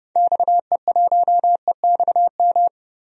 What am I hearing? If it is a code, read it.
Morse code, heard as a keyed tone, about 30 wpm, 700 Hz. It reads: XE1EXM
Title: DXCC-100 Call Signs - 6 Characters - Repeated 3x - 30wpm